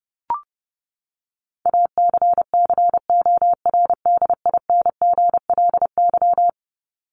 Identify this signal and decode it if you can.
Morse code, heard as a keyed tone, about 30 wpm, 700 Hz. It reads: ACCORDINGLY
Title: LY Suffix - Top 100 Words with Base Words 30wpm